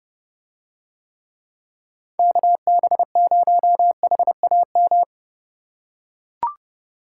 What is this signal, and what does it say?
Morse code, heard as a keyed tone, about 30 wpm, 700 Hz. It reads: KB0HAM